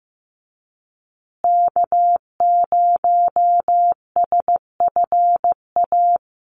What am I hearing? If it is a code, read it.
Morse code, heard as a keyed tone, about 15 wpm, 700 Hz. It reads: K0SFA
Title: USA Call Signs Repeated 2x 15wpm